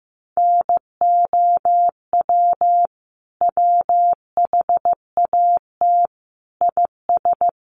Morse code, heard as a keyed tone, about 15 wpm, 700 Hz. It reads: NOW WHAT IS
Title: Sentences from Top 200 Words 15wpm